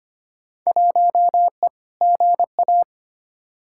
Morse code, heard as a keyed tone, about 25 wpm, 700 Hz. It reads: 1E GA
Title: Field Day Exchanges - 25wpm